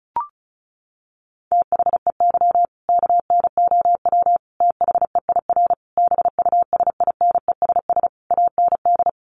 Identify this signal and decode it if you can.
Morse code, heard as a keyed tone, about 34 wpm, 700 Hz. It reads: THEY KNOW THEIR BUSINESS AND
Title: Sentences from Top 1200 Words 35wpm